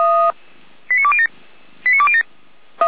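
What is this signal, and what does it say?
Five-tone selective call (ZVEI), heard as quick runs of five tones, tones of about 70 ms. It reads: 89207 89207
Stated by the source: ZVEI five-tone call received on SDR.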